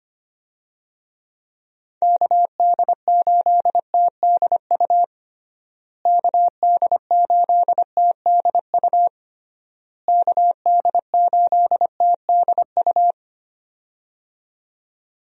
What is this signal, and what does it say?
Morse code, heard as a keyed tone, about 25 wpm, 700 Hz. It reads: KD8TDU KD8TDU KD8TDU
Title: USA Call Signs - 6 Characters - Repeated 3x - 25wpm